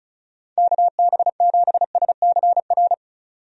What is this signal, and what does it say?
Morse code, heard as a keyed tone, about 34 wpm, 700 Hz. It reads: KB7SCR
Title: USA Call Signs - 6 Characters - 35wpm